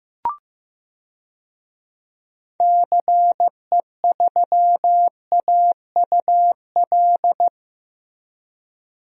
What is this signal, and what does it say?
Morse code, heard as a keyed tone, about 15 wpm, 700 Hz. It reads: CE3AUL
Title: DXCC-100 Call Signs - 6 Characters - Repeated 2x - 15wpm